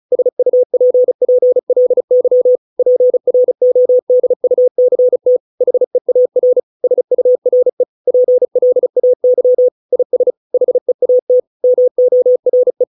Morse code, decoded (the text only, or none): SUPPLY PRODUCT HEAR SURE PLAY IS HEAT MORE